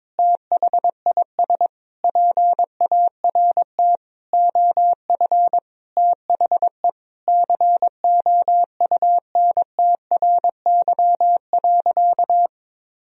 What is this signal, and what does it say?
Morse code, heard as a keyed tone, about 22 wpm, 700 Hz. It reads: THIS PART OF THE COUNTRY.
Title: Sentences from Top 300 Words and Abbreviations 22wpm